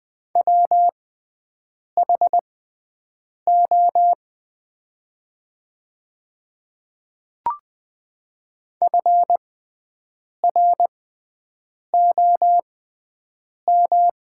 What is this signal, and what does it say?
Morse code, heard as a keyed tone, about 20 wpm, spaced as for 7 wpm, 700 Hz. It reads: WHO FROM
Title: 6x Character Spacing - Top 100 Words - 20wpm